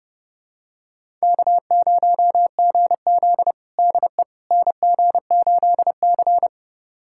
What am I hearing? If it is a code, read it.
Morse code, heard as a keyed tone, about 30 wpm, 700 Hz. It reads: K0GZ DE NG8C
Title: USA Callsign DE Callsign - 4 Characters - 30wpm